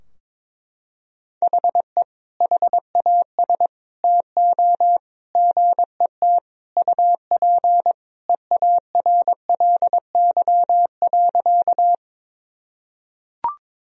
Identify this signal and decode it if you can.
Morse code, heard as a keyed tone, about 21 wpm, 700 Hz. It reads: HE HAS TO GET UP EARLY.
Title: Sentences from Top 1000 Words - Review Part 2 - 22wpm